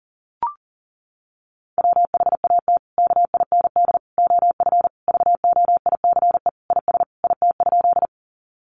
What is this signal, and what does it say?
Morse code, heard as a keyed tone, about 40 wpm, 700 Hz. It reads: WHAT KIND OF VOICE IS IT?
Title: Sentences from Top 500 Words 40wpm